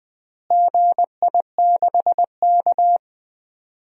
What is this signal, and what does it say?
Morse code, heard as a keyed tone, about 20 wpm, 700 Hz. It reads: GI6K
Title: DXCC-100 Call Signs - 4 Characters - Repeated 2x - 20wpm